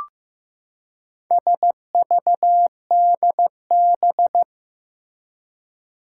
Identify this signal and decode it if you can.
Morse code, heard as a keyed tone, about 15 wpm, 700 Hz. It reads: SVDB
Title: Four Challenging Letter-Number Combinations 15wpm